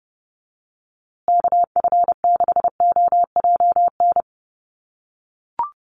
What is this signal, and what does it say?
Morse code, heard as a keyed tone, about 30 wpm, 700 Hz. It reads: KF6OJN
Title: USA Call Signs Repeated 2x 30wpm